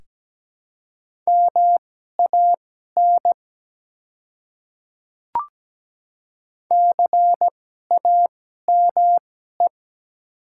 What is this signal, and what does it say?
Morse code, heard as a keyed tone, about 17 wpm, spaced as for 12 wpm, 700 Hz. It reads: MAN CAME
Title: 2x Character Spacing - Top 200 Words - 17wpm